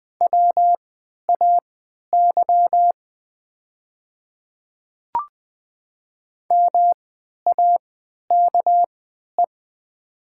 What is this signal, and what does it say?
Morse code, heard as a keyed tone, about 20 wpm, spaced as for 11 wpm, 700 Hz. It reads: WAY MAKE
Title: 3x Character Spacing - Top 200 Words - 20wpm